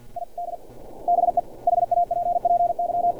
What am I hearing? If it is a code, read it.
Morse code, heard as a keyed tone, about 50 wpm, 700 Hz. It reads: ID BE HAPPY
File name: archived / CRSnet